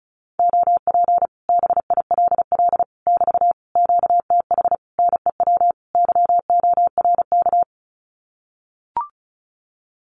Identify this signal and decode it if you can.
Morse code, heard as a keyed tone, about 34 wpm, 700 Hz. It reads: OP BILL <BT> QTH NEW YORK
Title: Common QSO Phrases Encore 35wpm